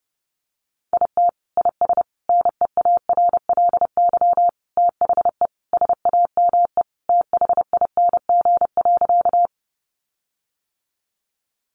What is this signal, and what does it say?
Morse code, heard as a keyed tone, about 30 wpm, 700 Hz. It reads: IT IS NEARLY THE SAME THING.
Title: Sentences from Top 1300 Words 30wpm